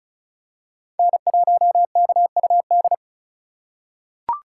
Morse code, heard as a keyed tone, about 36 wpm, 700 Hz. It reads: N1KUD
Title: USA Call Signs - 5 Characters - 4x Character Spacing - 35wpm